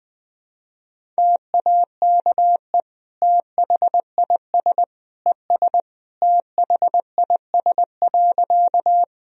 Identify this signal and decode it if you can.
Morse code, heard as a keyed tone, about 21 wpm, 700 Hz. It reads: TAKE THIS ES THIS.
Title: Sentences from Top 200 Words and Abbreviations 20wpm